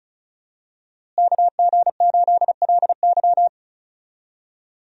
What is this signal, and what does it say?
Morse code, heard as a keyed tone, about 34 wpm, 700 Hz. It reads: KG8LY
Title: USA Call Signs - 5 Characters - Repeated 3x - 35wpm